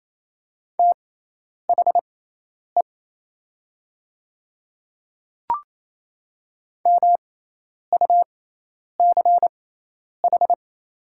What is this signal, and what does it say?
Morse code, heard as a keyed tone, about 28 wpm, spaced as for 10 wpm, 700 Hz. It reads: THE MUCH
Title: 6x Character Spacing - Top 300 Words - 28wpm